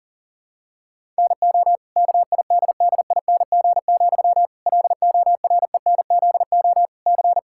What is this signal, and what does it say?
Morse code, heard as a keyed tone, about 40 wpm, 700 Hz. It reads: NO KIDDING, LORENZO C
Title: Pangram Challenge v2 - 40wpm